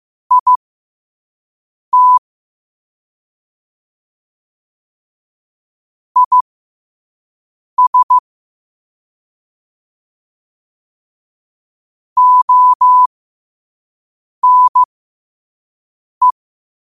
Morse code, heard as a keyed tone, about 15 wpm, spaced as for 5 wpm, 1000 Hz. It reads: IT IS ONE
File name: qotdfiles / 5wpm